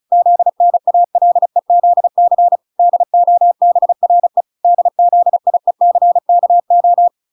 Morse code, heard as a keyed tone, about 34 wpm, 700 Hz. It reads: ZNALEZC DOBRE DZIECKO